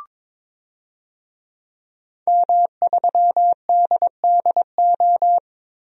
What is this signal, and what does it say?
Morse code, heard as a keyed tone, about 22 wpm, 700 Hz. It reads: M3DDO
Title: DXCC-100 Call Signs - 5 Characters - Repeated 3x - 22wpm